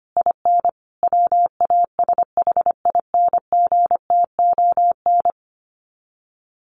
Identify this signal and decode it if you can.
Morse code, heard as a keyed tone, about 25 wpm, 700 Hz. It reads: IN WASHINGTON
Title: Sets of 5 Words Encore2 25wpm